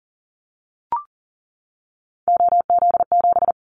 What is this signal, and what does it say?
Morse code, heard as a keyed tone, about 40 wpm, 700 Hz. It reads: OZ7
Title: Callsign Prefixes for Top 100 DXCC - 40wpm